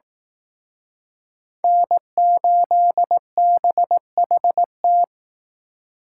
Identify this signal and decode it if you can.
Morse code, heard as a keyed tone, about 19 wpm, 700 Hz. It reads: N8BHT